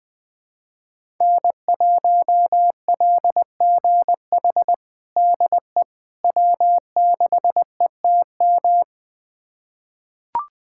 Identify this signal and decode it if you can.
Morse code, heard as a keyed tone, about 20 wpm, 700 Hz. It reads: N1LGH DE W6ETM
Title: USA Callsign DE Callsign - 5 Characters - 20wpm